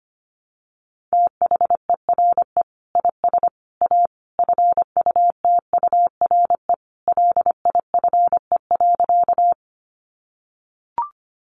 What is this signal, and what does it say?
Morse code, heard as a keyed tone, about 25 wpm, 700 Hz. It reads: THERE IS A FUTURE LIFE.
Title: Sentences from Top 700 Words 25wpm